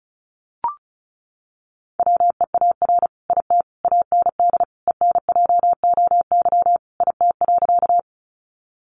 Morse code, heard as a keyed tone, about 34 wpm, 700 Hz. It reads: WEAR IT AND ENJOY IT.
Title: Sentences from Top 1500 Words 35wpm